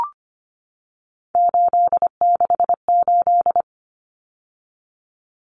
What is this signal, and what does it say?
Morse code, heard as a keyed tone, about 25 wpm, 700 Hz. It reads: 868
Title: Numbers - 3-Digit 25wpm